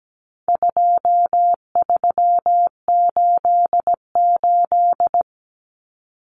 Morse code, heard as a keyed tone, about 17 wpm, 700 Hz. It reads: 2388
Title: Numbers - 4-Digit 17wpm